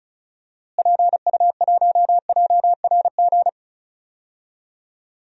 Morse code, heard as a keyed tone, about 34 wpm, 700 Hz. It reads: PU1JRG
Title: DXCC-100 Call Signs - 6 Characters - Repeated 2x - 35wpm